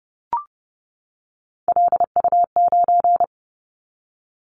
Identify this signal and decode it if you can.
Morse code, heard as a keyed tone, about 30 wpm, 700 Hz. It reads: LU9
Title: Callsign Prefixes for Top 100 DXCC - 30wpm